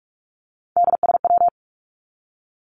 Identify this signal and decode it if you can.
Morse code, heard as a keyed tone, about 47 wpm, 700 Hz. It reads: DSW
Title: Abbreviations 45wpm